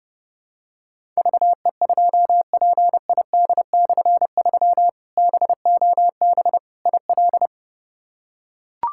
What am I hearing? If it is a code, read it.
Morse code, heard as a keyed tone, about 30 wpm, 700 Hz. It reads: VE2PID/3 BOB IL
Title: K1USN SST QSO Practice - 30wpm